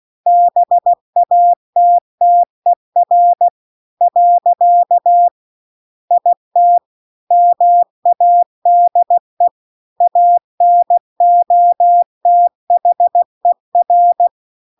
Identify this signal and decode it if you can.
Morse code, heard as a keyed tone, about 16 wpm, 700 Hz. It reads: BATTER . IT MADE ANOTHER